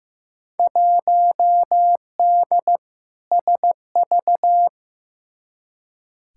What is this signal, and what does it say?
Morse code, heard as a keyed tone, about 15 wpm, 700 Hz. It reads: 1D SV
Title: Field Day Exchanges - 15wpm